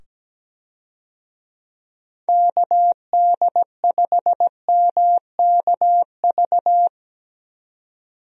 Morse code, heard as a keyed tone, about 17 wpm, 700 Hz. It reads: KD5MKV
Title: USA Call Signs - 6 Characters - 17wpm